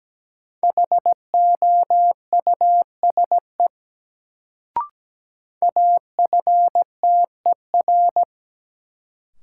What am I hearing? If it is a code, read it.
Morse code, heard as a keyed tone, about 17 wpm, 700 Hz. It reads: HOUSE AFTER
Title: Sets of 2 Words - New Word Drill - 17wpm